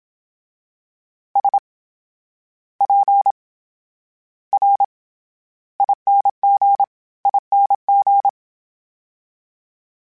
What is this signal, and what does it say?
Morse code, heard as a keyed tone, about 27 wpm, spaced as for 12 wpm, 800 Hz. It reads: SPRINGING
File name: Special INGs / Special INGs-mp3s / VerySlowToFaster Ing @25